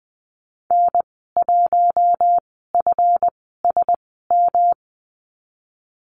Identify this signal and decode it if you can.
Morse code, heard as a keyed tone, about 20 wpm, spaced as for 15 wpm, 700 Hz. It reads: N1FSM
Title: 173 Lesson 28g - Number 9 - Call Signs 20wpm